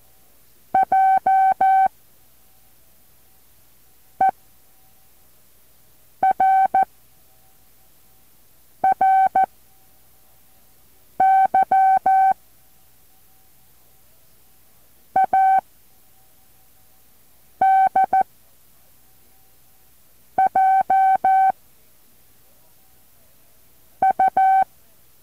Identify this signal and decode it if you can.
Morse code, heard as a keyed tone, about 14 wpm, 770 Hz. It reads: J E R R Y A D J U
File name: intl morse code2